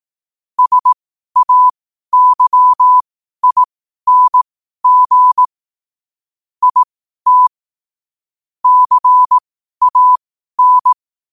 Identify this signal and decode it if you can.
Morse code, heard as a keyed tone, about 18 wpm, spaced as for 12 wpm, 1000 Hz. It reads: SAYING IT CAN
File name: qotdfiles / 12wpm